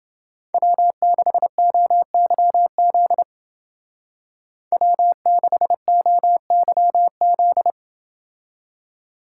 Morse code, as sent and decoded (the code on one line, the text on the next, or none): .-- -.... --- -.-- --.. / .-- -.... --- -.-- --..
W6OYZ W6OYZ